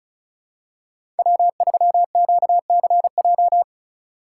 Morse code, heard as a keyed tone, about 36 wpm, 690 Hz. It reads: W3QCJ
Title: USA Call Signs - 5 Characters - 2x Character Spacing - 35wpm